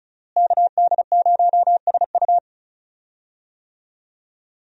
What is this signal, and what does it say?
Morse code, heard as a keyed tone, about 34 wpm, 700 Hz. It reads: KD0SU
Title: USA Call Signs - 5 Characters - Repeated 2x - 35wpm